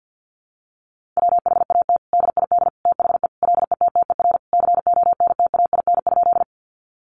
Morse code, heard as a keyed tone, about 50 wpm, 700 Hz. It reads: WHAT DID THE LETTER CONTAIN?
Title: Sentences from Top 500 Words 50wpm